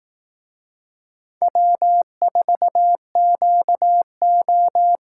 Morse code, heard as a keyed tone, about 18 wpm, 700 Hz. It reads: W4QO
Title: Speed Racing - USA 1x2 Call Signs 15wpm